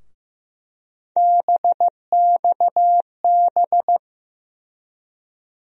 Morse code, heard as a keyed tone, about 15 wpm, 700 Hz. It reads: BXB
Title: USA Call Sign Postfixes - 3 Letters 15wpm